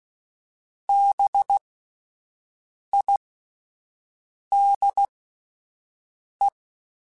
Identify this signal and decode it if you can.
Morse code, heard as a keyed tone, about 16 wpm, spaced as for 5 wpm, 800 Hz. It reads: BIDE
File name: Spaced Alphas / Spaced Alphas mp3s / Bs sp @10